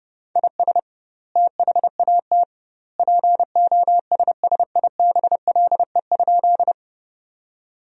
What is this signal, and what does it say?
Morse code, heard as a keyed tone, about 30 wpm, 700 Hz. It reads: IS THAT POSSIBLE?
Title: 2x Word-Spacing - Sentences from Top 500 Words - 30wpm